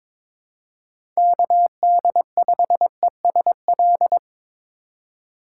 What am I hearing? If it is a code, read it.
Morse code, heard as a keyed tone, about 21 wpm, 700 Hz. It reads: KD5ESL
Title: USA Call Signs - 6 Characters - 4x Character Spacing - 22wpm